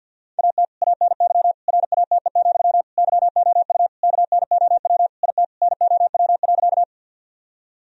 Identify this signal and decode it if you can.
Morse code, heard as a keyed tone, about 50 wpm, 700 Hz. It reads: AT ANY RATE, YOU KNOW IT NOW.
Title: Sentences from Top 1000 Words - Review Part 4 - 50wpm